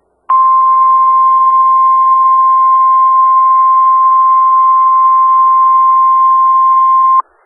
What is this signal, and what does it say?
Emergency Alert System (SAME) tones (the 1050 Hz attention tone). It is heard from 0.3 to 7.3 s.
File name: Skywarn/2013/2013-03-18-Newton County WA4ASI Repeater